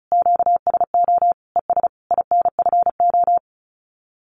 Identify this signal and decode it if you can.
Morse code, heard as a keyed tone, about 34 wpm, 700 Hz. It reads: QSO ES INFO
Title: Common QSO Phrases Encore 35wpm